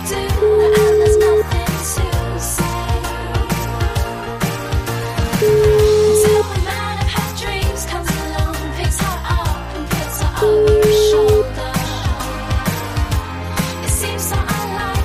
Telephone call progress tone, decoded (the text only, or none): ringback tone